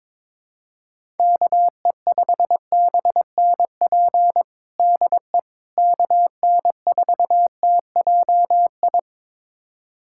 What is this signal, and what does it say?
Morse code, heard as a keyed tone, about 22 wpm, 700 Hz. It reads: KE5BNP DE KN4TJI
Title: USA Callsign DE Callsign 22wpm